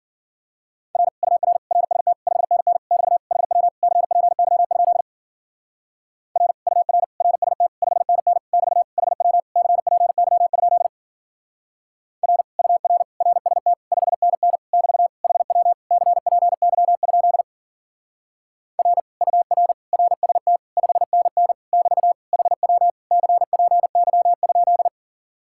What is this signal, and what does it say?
Morse code, heard as a keyed tone, about 51 wpm, 700 Hz. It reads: R UR RST 5NN <BT> HW CPY? R UR RST 5NN <BT> HW CPY? R UR RST 5NN <BT> HW CPY? R UR RST 5NN <BT> HW CPY?